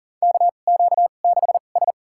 Morse code, heard as a keyed tone, about 40 wpm, spaced as for 29 wpm, 700 Hz. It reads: KQ6S
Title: USA Call Signs - 4 Characters - 2x Character Spacing - 40wpm